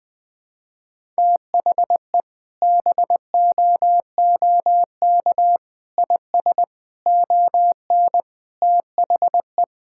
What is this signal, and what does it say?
Morse code, heard as a keyed tone, about 20 wpm, 700 Hz. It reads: THE BOOK IS ON THE
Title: Sentences with 3-Word Phrases - 20wpm